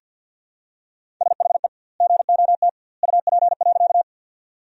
Morse code, heard as a keyed tone, about 50 wpm, 700 Hz. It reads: SHE GOT UP.
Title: 2x Word-Spacing - Sentences from Top 300 Words - 50wpm